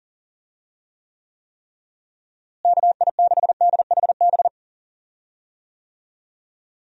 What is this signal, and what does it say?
Morse code, heard as a keyed tone, about 37 wpm, 700 Hz. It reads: KI6DHB
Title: USA Call Signs - 6 Characters - Repeated 3x - 40wpm